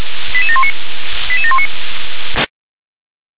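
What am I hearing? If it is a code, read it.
Five-tone selective call (ZVEI), heard as quick runs of five tones, tones of about 70 ms. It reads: 88719 88719
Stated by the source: ZVEI five-tone selective call received by SDR.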